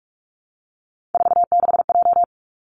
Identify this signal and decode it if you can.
Morse code, heard as a keyed tone, about 46 wpm, 700 Hz. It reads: 46J
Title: Three Challenging Letter-Number Combinations 45wpm